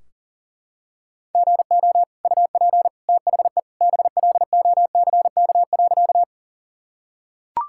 Morse code, heard as a keyed tone, about 40 wpm, 700 Hz. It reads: GO UP THE BLOCK.
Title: Sentences from Top 1600 Words 40wpm